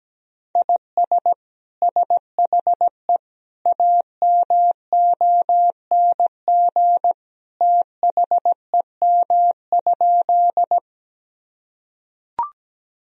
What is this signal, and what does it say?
Morse code, heard as a keyed tone, about 17 wpm, 700 Hz. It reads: IS SHE AMONG THEM?
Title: Sentences from Top 1000 Words - Review Part 3 - 17wpm